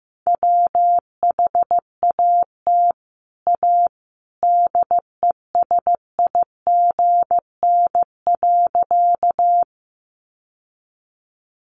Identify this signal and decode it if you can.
Morse code, heard as a keyed tone, about 15 wpm, 700 Hz. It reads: WHAT A DESIGN.
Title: Sentences from Top 800 Words 15wpm